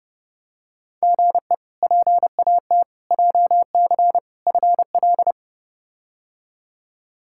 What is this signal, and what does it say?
Morse code, heard as a keyed tone, about 30 wpm, 700 Hz. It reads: GE PAT JC FL
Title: K1USN SST QSO Practice - 30wpm